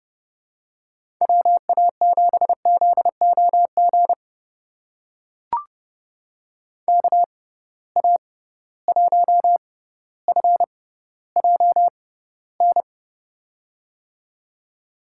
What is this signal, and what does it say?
Morse code, heard as a keyed tone, about 30 wpm, spaced as for 15 wpm, 700 Hz. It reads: WA7ZOG KA1FJN
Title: USA Call Signs - 6 Characters - 6x Character Spacing - 30wpm